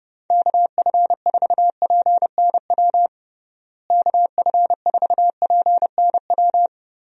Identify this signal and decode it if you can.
Morse code, heard as a keyed tone, about 30 wpm, 700 Hz. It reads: KF4PNW KF4PNW
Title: USA Call Signs - 6 Characters - Repeated 2x - 30wpm